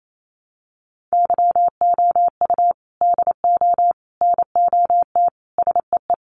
Morse code, heard as a keyed tone, about 28 wpm, 700 Hz. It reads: YOU DO NOT SEE
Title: Sentences from Top 1300 Words 28wpm